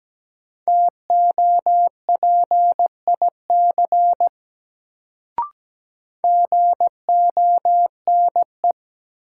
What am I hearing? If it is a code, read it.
Morse code, heard as a keyed tone, about 17 wpm, 700 Hz. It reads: TOPIC GONE
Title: Sentences from Top 1200 Words - New Word Drill - 17wpm